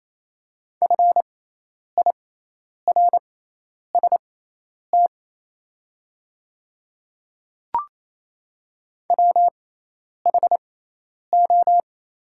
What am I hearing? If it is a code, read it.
Morse code, heard as a keyed tone, about 28 wpm, spaced as for 10 wpm, 700 Hz. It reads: FIRST WHO
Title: 6x Character Spacing - Top 100 Words - 28wpm